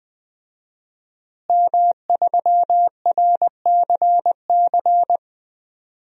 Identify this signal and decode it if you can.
Morse code, heard as a keyed tone, about 20 wpm, 700 Hz. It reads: M3RCC
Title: DXCC-100 Call Signs - 5 Characters - Repeated 2x - 20wpm